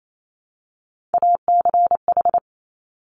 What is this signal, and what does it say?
Morse code, heard as a keyed tone, about 28 wpm, 700 Hz. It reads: ACH
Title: Top Three Letter Combinations in Words 28wpm